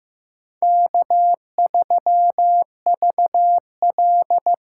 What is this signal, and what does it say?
Morse code, heard as a keyed tone, about 18 wpm, 700 Hz. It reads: K3VL
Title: Speed Racing - USA 1x2 Call Signs 15wpm